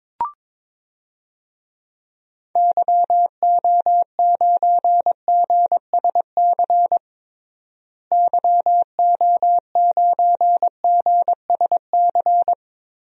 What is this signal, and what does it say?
Morse code, heard as a keyed tone, about 23 wpm, 700 Hz. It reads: YO9GSC YO9GSC
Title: DXCC-100 Call Signs - 6 Characters - Repeated 3x - 22wpm